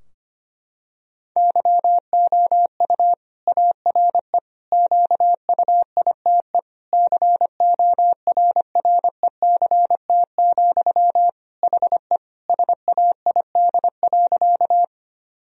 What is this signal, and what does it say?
Morse code, heard as a keyed tone, about 25 wpm, 700 Hz. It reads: YOU ARE QUITE CORRECT, HE SAID.
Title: Sentences from Top 1000 Words - Review Part 2 - 25wpm